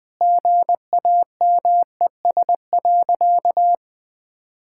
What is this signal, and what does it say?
Morse code, heard as a keyed tone, about 20 wpm, 700 Hz. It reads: GAMES.
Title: Sentences with 5-Word Phrases - Spoken Beginning - 20wpm